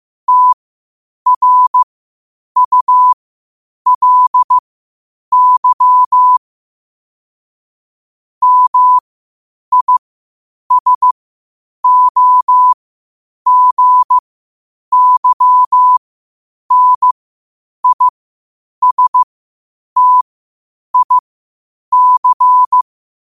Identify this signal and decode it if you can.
Morse code, heard as a keyed tone, about 15 wpm, spaced as for 8 wpm, 1000 Hz. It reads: TRULY MISOGYNISTIC
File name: qotdfiles / 8wpm